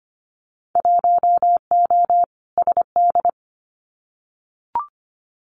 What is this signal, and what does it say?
Morse code, heard as a keyed tone, about 25 wpm, 700 Hz. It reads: 1O SD
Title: Winter Field Day Exchanges Encore 25wpm